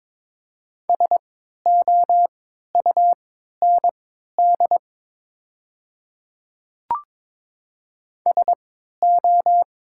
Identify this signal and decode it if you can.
Morse code, heard as a keyed tone, about 22 wpm, spaced as for 13 wpm, 700 Hz. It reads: SOUND SO
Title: 3x Character Spacing - Top 100 Words - 22wpm